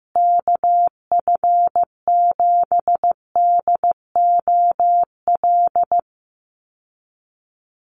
Morse code, heard as a keyed tone, about 15 wpm, 700 Hz. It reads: KF7DOL
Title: USA Call Signs Repeated 2x 15wpm